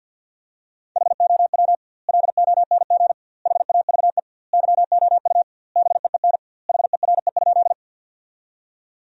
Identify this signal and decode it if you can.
Morse code, heard as a keyed tone, about 50 wpm, 700 Hz. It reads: HOW LONG HAVE YOU BEEN HERE?
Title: 2x Word-Spacing - Sentences from Top 200 Words - 50wpm